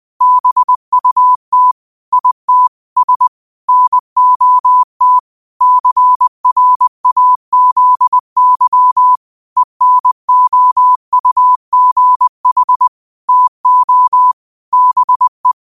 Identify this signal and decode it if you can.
Morse code, heard as a keyed tone, about 20 wpm, 1000 Hz. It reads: BUT IT S NOT CRAZY ENOUGH TO BE